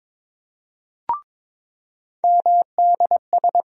Morse code, heard as a keyed tone, about 22 wpm, 700 Hz. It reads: MDS
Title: USA Call Sign Postfixes - 3 Letters 22wpm